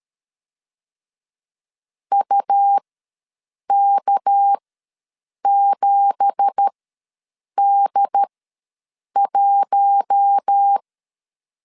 Morse code, heard as a keyed tone, about 13 wpm, spaced as for 7 wpm, 800 Hz. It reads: UK7D1
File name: LEZIONI / 9 2867 ws 12wpm 3431